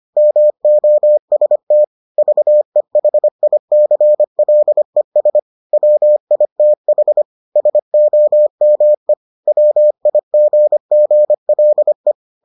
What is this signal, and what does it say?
Morse code, heard as a keyed tone, about 25 wpm, 600 Hz. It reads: MOST VEHICLES WITH SOME WIGGLE